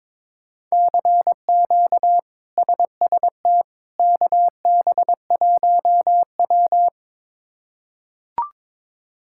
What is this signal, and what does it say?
Morse code, heard as a keyed tone, about 22 wpm, 700 Hz. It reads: CQ SST KB1W
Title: K1USN SST QSO Practice - 22wpm